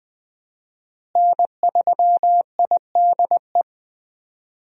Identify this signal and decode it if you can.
Morse code, heard as a keyed tone, about 20 wpm, 700 Hz. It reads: N3IDE
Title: USA Call Signs - 5 Characters - Repeated 2x - 20wpm